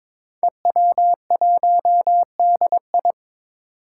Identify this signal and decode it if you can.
Morse code, heard as a keyed tone, about 22 wpm, 700 Hz. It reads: EW1DI
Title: Five Challenging Letter-Number Combinations 22wpm